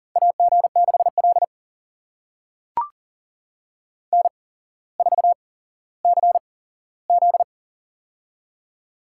Morse code, heard as a keyed tone, about 38 wpm, spaced as for 17 wpm, 700 Hz. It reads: AG6L N4CZ
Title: USA Call Signs - 4 Characters - 8x Character Spacing - 40wpm